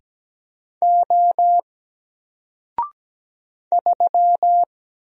Morse code, heard as a keyed tone, about 17 wpm, 700 Hz. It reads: O 3
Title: Single Character (Letter:Number:Punctuation:Prosign) Alt 17wpm